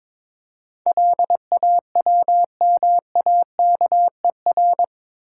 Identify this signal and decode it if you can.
Morse code, heard as a keyed tone, about 22 wpm, 700 Hz. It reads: LAWMAKER
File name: practice / 22wpm / Eight Letter Words - 22wpm